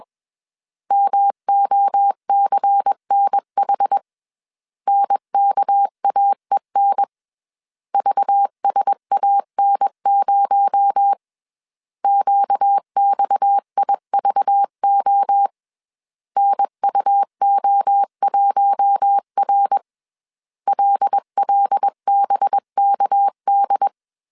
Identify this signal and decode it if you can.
Morse code, heard as a keyed tone, about 21 wpm, 800 Hz. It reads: MOCNH NKAEN VSAN0 QXIVO NUO1R LLBKD
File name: CW / DET / 1 6 1500C S B 20wpm 2048